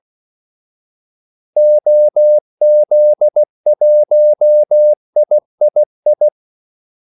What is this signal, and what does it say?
Morse code, heard as a keyed tone, about 16 wpm, 600 Hz. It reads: OZ1III
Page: Kaldesignaler 16 wpm